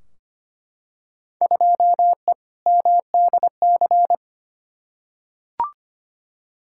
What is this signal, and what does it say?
Morse code, heard as a keyed tone, about 25 wpm, 700 Hz. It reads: 2E MDC
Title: Field Day Exchanges - 25wpm